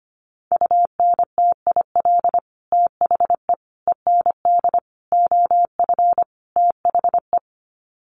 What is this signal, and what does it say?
Morse code, heard as a keyed tone, about 25 wpm, 700 Hz. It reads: UNTIL THE END OF THE
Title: Sets of 5 Words Encore 25wpm